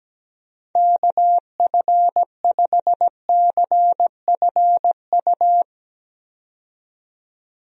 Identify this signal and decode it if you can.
Morse code, heard as a keyed tone, about 17 wpm, 700 Hz. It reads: KF5CFU
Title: USA Call Signs - 6 Characters - 17wpm